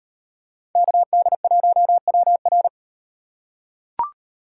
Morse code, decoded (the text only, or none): KD1WR